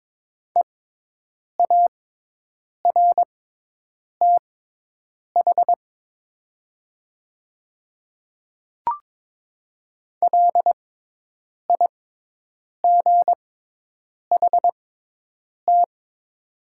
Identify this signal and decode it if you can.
Morse code, heard as a keyed tone, about 22 wpm, spaced as for 8 wpm, 700 Hz. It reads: EARTH LIGHT
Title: 6x Character Spacing - Top 200 Words - 22wpm